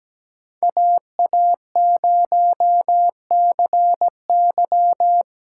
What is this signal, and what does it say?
Morse code, heard as a keyed tone, about 20 wpm, 700 Hz. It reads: AA0CY
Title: Speed Racing - USA Call Signs 17wpm